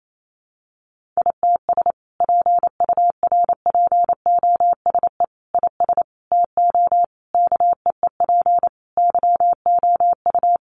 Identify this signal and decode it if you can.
Morse code, heard as a keyed tone, about 28 wpm, 700 Hz. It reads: ITS PURPOSE IS TO KEEP YOU
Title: A Sound Of Thunder 28wpm